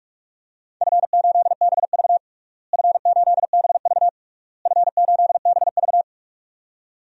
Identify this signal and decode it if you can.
Morse code, heard as a keyed tone, about 44 wpm, 700 Hz. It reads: F8BV F8BV F8BV
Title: DXCC-100 Call Signs - 4 Characters - Repeated 3x - 45wpm